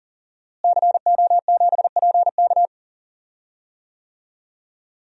Morse code, heard as a keyed tone, about 38 wpm, 700 Hz. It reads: CO7PK
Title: DXCC-100 Call Signs - 5 Characters - Repeated 2x - 40wpm